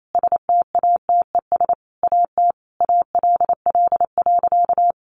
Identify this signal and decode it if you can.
Morse code, heard as a keyed tone, about 28 wpm, 700 Hz. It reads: STATES AT ALL.
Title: A Sound Of Thunder 28wpm